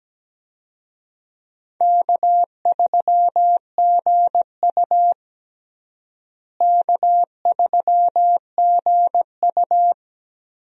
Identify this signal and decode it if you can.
Morse code, heard as a keyed tone, about 17 wpm, 700 Hz. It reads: K3GU K3GU
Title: USA Call Signs - 4 Characters - Repeated 3x - 17wpm